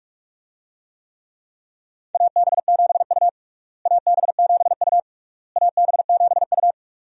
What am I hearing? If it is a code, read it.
Morse code, heard as a keyed tone, about 46 wpm, 700 Hz. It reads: AB7U AB7U AB7U
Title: USA Call Signs - 4 Characters - Repeated 3x - 45wpm